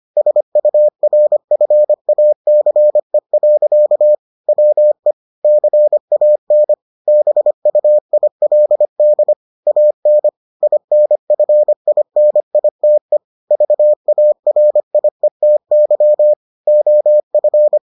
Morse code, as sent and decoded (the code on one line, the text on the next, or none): ... ..- .-. ..-. .- -.-. .-.-.- / .-- . / -.-. .- -. / -... ..- .. .-.. -.. / .- -. / .. -. ..-. .. -. .. - . / ...- .- .-. .. - -.-- / --- ..-.
SURFACE. WE CAN BUILD AN INFINITE VARIETY OF